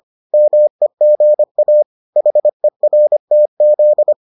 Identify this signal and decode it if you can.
Morse code, heard as a keyed tone, about 25 wpm, 600 Hz. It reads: MEGA HERTZ